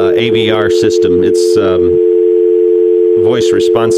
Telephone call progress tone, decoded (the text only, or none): dial tone